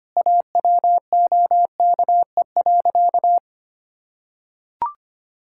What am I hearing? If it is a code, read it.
Morse code, heard as a keyed tone, about 25 wpm, 700 Hz. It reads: AWOKE.
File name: practice / 25wpm / Sentences with 4-Word Phrases - 25wpm